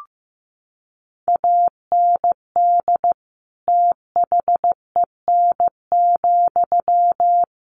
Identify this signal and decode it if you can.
Morse code, heard as a keyed tone, about 15 wpm, 700 Hz. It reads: AND THEN,
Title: Sentences from Top 1500 Words 15wpm